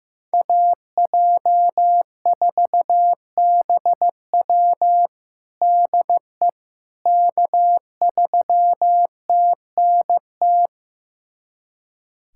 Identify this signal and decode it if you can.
Morse code, heard as a keyed tone, about 15 wpm, 700 Hz. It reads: AJ4BW DE K3TNT
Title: USA Callsign DE Callsign - 5 Characters - 15wpm